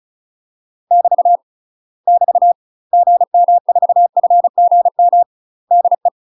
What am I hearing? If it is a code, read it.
Morse code, heard as a keyed tone, about 35 wpm, 700 Hz. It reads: <BT> <BT> GM4FGM DE
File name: UK MASTER QSOs TRAINING / 35WPM / UK MASTER QSOs 0007